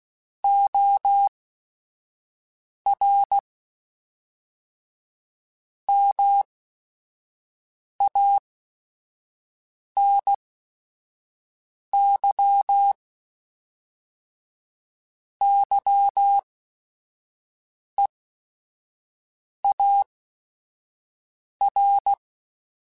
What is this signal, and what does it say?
Morse code, heard as a keyed tone, about 16 wpm, spaced as for 5 wpm, 800 Hz. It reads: OR MANY YEAR
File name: Technical Subjects / Technical Subjects-mp3s / AND Gate @5